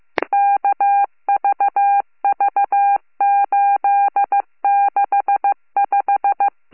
Morse code, heard as a keyed tone, about 15 wpm, 800 Hz. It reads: KVV865